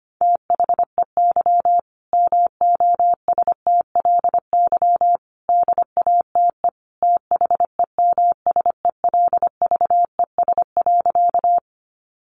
Morse code, heard as a keyed tone, about 25 wpm, 700 Hz. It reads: THEY MOSTLY DATE THEMSELVES.
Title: Sentences from Top 1300 Words 25wpm